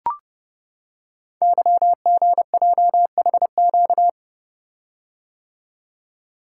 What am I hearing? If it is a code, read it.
Morse code, heard as a keyed tone, about 29 wpm, 700 Hz. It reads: YGJHQ
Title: Code Groups - 5 Letters - 30wpm